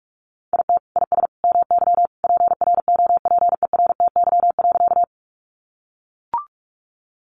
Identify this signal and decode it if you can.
Morse code, heard as a keyed tone, about 44 wpm, 700 Hz. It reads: IT IS MY PROPERTY.
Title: Sentences from Top 1000 Words 45wpm